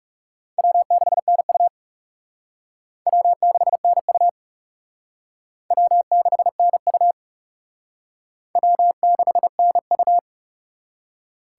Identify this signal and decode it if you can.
Morse code, heard as a keyed tone, about 35 wpm, 700 Hz. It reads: W6NU W6NU W6NU W6NU
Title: Speed Racing - USA 1x2 Call Signs 30wpm